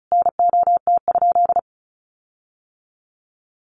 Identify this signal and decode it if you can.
Morse code, heard as a keyed tone, about 34 wpm, 700 Hz. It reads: NOT?
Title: Sentences from Top 1300 Words 35wpm